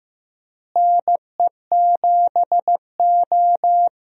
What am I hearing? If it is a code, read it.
Morse code, heard as a keyed tone, about 16 wpm, 700 Hz. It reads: NE7O
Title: USA Call Signs - 4 Characters - Repeated 2x - 15wpm